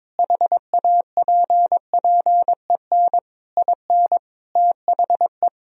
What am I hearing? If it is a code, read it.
Morse code, heard as a keyed tone, about 22 wpm, 700 Hz. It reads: HAPPEN IN THE
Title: Sentences from Top 1000 Words - Review Part 2 - 22wpm